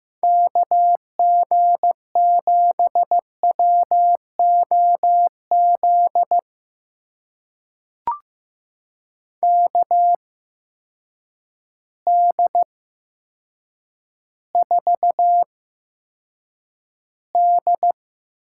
Morse code, heard as a keyed tone, about 15 wpm, spaced as for 7 wpm, 700 Hz. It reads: KG7WOZ KD4D
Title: USA Call Signs - 6 Characters - 8x Character Spacing - 15wpm